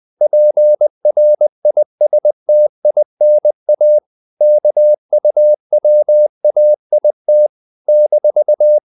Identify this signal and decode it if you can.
Morse code, heard as a keyed tone, about 20 wpm, 600 Hz. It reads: PRISTINA KUWAIT -
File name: CW / Practice / NationalCapitals20